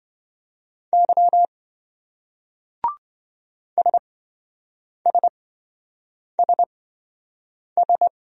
Morse code, heard as a keyed tone, about 27 wpm, spaced as for 7 wpm, 700 Hz. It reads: Y SSSS